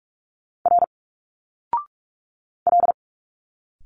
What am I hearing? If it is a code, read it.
Morse code, heard as a keyed tone, about 46 wpm, 700 Hz. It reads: R L
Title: Single Letters 45wpm